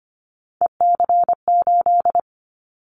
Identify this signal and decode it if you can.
Morse code, heard as a keyed tone, about 25 wpm, 700 Hz. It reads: EC8
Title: Callsign Prefixes for Top 100 DXCC - 25wpm